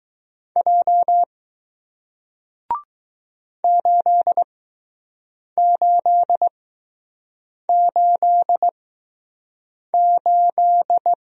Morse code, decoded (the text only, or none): J 8888